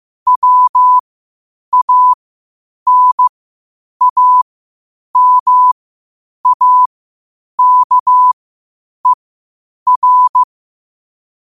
Morse code, heard as a keyed tone, about 15 wpm, spaced as for 8 wpm, 1000 Hz. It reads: WANAMAKER
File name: qotdfiles / 8wpm